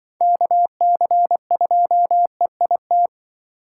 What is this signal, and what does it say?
Morse code, heard as a keyed tone, about 24 wpm, 700 Hz. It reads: KC2EIT
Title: Speed Racing - USA 2x3 Call Signs 20wpm